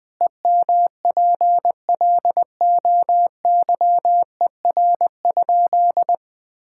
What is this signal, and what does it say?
Morse code, heard as a keyed tone, about 20 wpm, 700 Hz. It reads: EMPLOYER?
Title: Sentences from Top 1600 Words 20wpm